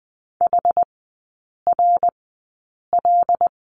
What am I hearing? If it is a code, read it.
Morse code, heard as a keyed tone, about 20 wpm, spaced as for 8 wpm, 690 Hz. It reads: HRL
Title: 081 Lesson 14e - Number 2 - Three Characters 20wpm